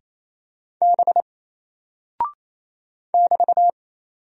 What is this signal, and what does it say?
Morse code, heard as a keyed tone, about 27 wpm, 700 Hz. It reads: B <BT>
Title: Single Character (Letter:Number:Punctuation:Prosign) 28wpm